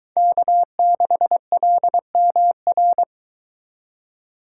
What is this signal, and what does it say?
Morse code, heard as a keyed tone, about 23 wpm, 700 Hz. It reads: K6LMR